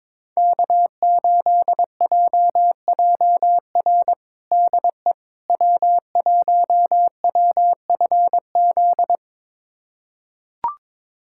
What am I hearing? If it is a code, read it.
Morse code, heard as a keyed tone, about 22 wpm, 700 Hz. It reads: K8JJR DE W1WFZ
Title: USA Callsign DE Callsign - 5 Characters - 22wpm